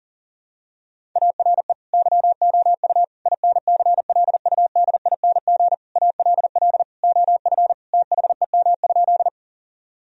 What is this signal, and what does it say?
Morse code, heard as a keyed tone, about 40 wpm, 700 Hz. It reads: ARE YOU INCLUDING ALL OF THEM?
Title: Sentences from Top 1000 Words - Review Part 4 - 40wpm